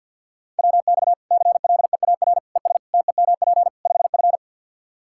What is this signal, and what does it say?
Morse code, heard as a keyed tone, about 51 wpm, 700 Hz. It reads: WX CLEAR ES TEMP 5F
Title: Practice QSO - Exchange 2 - Info - 50wpm